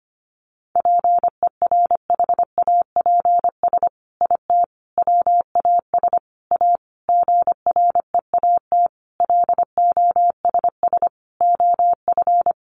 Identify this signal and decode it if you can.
Morse code, heard as a keyed tone, about 25 wpm, 700 Hz. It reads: PERHAPS IT WAS A GREAT LOSS OF
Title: Sentences from Top 1400 Words 25wpm